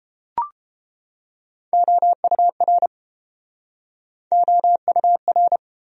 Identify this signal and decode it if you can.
Morse code, heard as a keyed tone, about 30 wpm, 700 Hz. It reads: OUR OUR